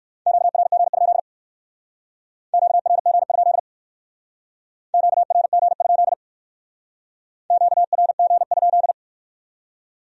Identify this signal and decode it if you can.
Morse code, heard as a keyed tone, about 55 wpm, 700 Hz. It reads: QRG? QRG? QRG? QRG?
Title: Speed-Racing Q-Codes - 45wpm